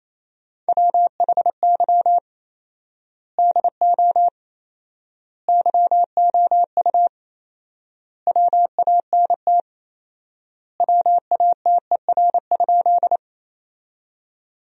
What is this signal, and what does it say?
Morse code, heard as a keyed tone, about 28 wpm, 700 Hz. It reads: WHY DO YOU WANT WATER?
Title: 4x Word-Spacing - Sentences from Top 200 Words - 28wpm